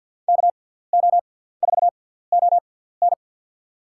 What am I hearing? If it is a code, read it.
Morse code, heard as a keyed tone, about 50 wpm, spaced as for 17 wpm, 700 Hz. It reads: KO4ON
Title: USA Call Signs - 5 Characters - 6x Character Spacing - 50wpm